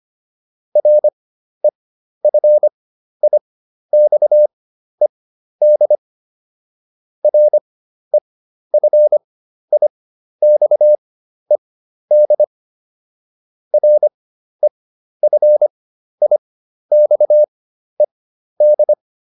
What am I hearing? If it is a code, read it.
Morse code, heard as a keyed tone, about 25 wpm, spaced as for 12 wpm, 600 Hz. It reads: REFIXED REFIXED REFIXED